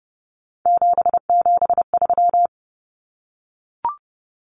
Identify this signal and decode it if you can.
Morse code, heard as a keyed tone, about 30 wpm, 700 Hz. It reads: 773
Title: Common Cut Numbers - 3-Digit 30wpm